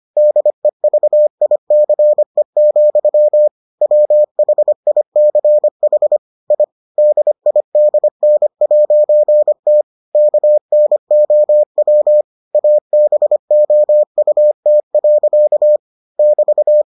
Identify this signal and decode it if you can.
Morse code, heard as a keyed tone, about 25 wpm, 600 Hz. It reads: DEVICE, WHICH I DIDN'T KNOW ABOUT. <BT>